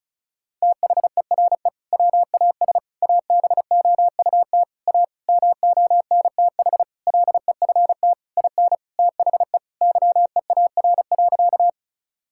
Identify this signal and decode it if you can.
Morse code, heard as a keyed tone, about 36 wpm, 700 Hz. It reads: THERE WAS ABOUT A MONTH LEFT IN THE YEAR.
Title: Sentences from Top 1000 Words - Review Part 5 - 35wpm